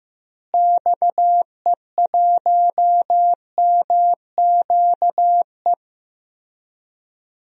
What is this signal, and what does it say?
Morse code, heard as a keyed tone, about 15 wpm, 700 Hz. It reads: XE1MQE
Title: DXCC-100 Call Signs - 6 Characters - Repeated 3x - 15wpm